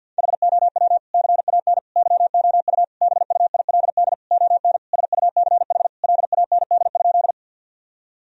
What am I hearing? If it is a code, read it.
Morse code, heard as a keyed tone, about 52 wpm, 700 Hz. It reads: HOW CAN YOU BUILD ON SUCH LAND?
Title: Sentences from Top 1000 Words - Review Part 1 - 50wpm